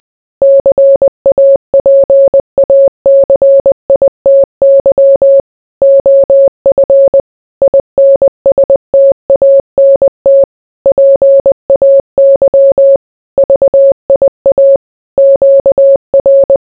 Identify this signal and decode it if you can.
Morse code, heard as a keyed tone, about 20 wpm, 550 Hz. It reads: CAPACITY OF INSTANT PAY VIA QR